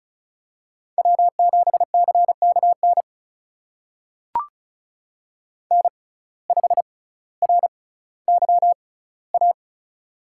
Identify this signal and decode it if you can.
Morse code, heard as a keyed tone, about 36 wpm, spaced as for 18 wpm, 700 Hz. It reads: W7CKN N5RYA
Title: USA Call Signs - 5 Characters - 6x Character Spacing - 35wpm